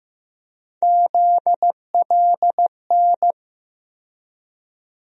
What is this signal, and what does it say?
Morse code, heard as a keyed tone, about 15 wpm, 700 Hz. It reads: ZLN
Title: USA Call Sign Postfixes - 3 Letters 15wpm